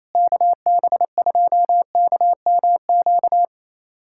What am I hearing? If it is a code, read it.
Morse code, heard as a keyed tone, about 28 wpm, 700 Hz. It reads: KB2KMQ
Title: USA Callsign DE Callsign - 6 Characters - 28wpm